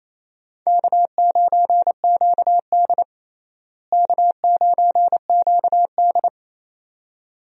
Morse code, heard as a keyed tone, about 28 wpm, 700 Hz. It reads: K9QD K9QD
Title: USA Call Signs - 4 Characters - Repeated 2x - 28wpm